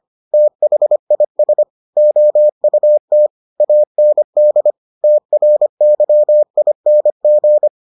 Morse code, heard as a keyed tone, about 25 wpm, 600 Hz. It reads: THIS OUT AND TRYING